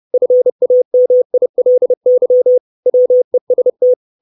Morse code, heard as a keyed tone, about 30 wpm, 500 Hz. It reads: FAMILY WEST
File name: 632 Words 30WPM